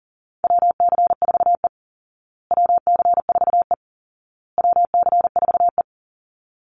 Morse code, heard as a keyed tone, about 40 wpm, 700 Hz. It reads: WC4E WC4E WC4E
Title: Call Signs Repeated 2x - 40wpm